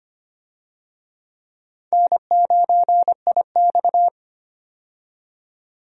Morse code, heard as a keyed tone, about 25 wpm, 700 Hz. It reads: N9IX
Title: USA Call Signs - 4 Characters - 25wpm